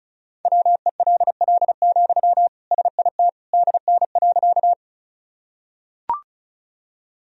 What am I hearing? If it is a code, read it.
Morse code, heard as a keyed tone, about 36 wpm, 700 Hz. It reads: WELL, SIT DN.
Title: Sentences from Top 500 Words and Abbreviations 35wpm